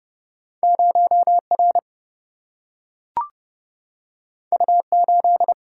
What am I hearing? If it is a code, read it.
Morse code, heard as a keyed tone, about 30 wpm, 700 Hz. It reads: 0R U8
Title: Single-Letter Single-Number 30wpm